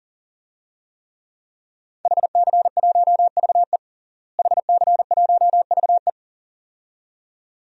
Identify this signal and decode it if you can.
Morse code, heard as a keyed tone, about 40 wpm, 700 Hz. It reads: HC1VE HC1VE
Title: DXCC-100 Call Signs - 5 Characters - Repeated 2x - 40wpm